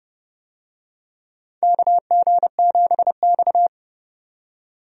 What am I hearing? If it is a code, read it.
Morse code, heard as a keyed tone, about 31 wpm, 700 Hz. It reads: KG7X
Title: USA Call Signs - 4 Characters - 30wpm